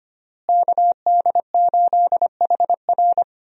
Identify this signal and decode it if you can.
Morse code, heard as a keyed tone, about 25 wpm, 700 Hz. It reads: KD8HR
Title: USA Call Signs - 5 Characters - Repeated 3x - 25wpm